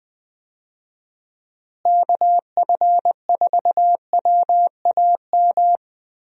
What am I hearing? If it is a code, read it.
Morse code, heard as a keyed tone, about 20 wpm, 700 Hz. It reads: KF4WAM
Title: USA Call Signs - 6 Characters - Repeated 2x - 20wpm